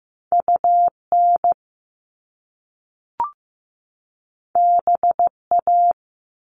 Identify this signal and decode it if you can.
Morse code, heard as a keyed tone, about 15 wpm, 700 Hz. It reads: UN BA
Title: Top Two Letter Combinations in Words 15wpm